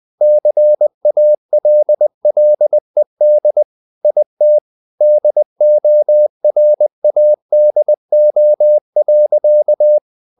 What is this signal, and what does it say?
Morse code, heard as a keyed tone, about 20 wpm, 600 Hz. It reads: CALLED IT DORADO.
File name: the old man and the sea part 2 20wpm